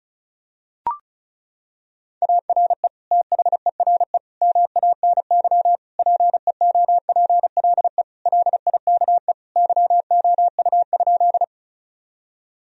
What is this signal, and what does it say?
Morse code, heard as a keyed tone, about 36 wpm, 700 Hz. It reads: ARE THERE MANY PEOPLE LIKE YOU?
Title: Sentences from Top 1000 Words - Review Part 1 - 35wpm